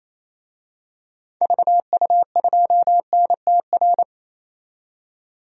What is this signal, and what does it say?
Morse code, heard as a keyed tone, about 28 wpm, 700 Hz. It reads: VU2NTR
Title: DXCC-100 Call Signs - 6 Characters - 28wpm